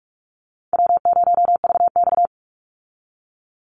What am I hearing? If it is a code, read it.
Morse code, heard as a keyed tone, about 44 wpm, 700 Hz. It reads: W0VX
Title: Easy Call Signs Repeated 3x - 45wpm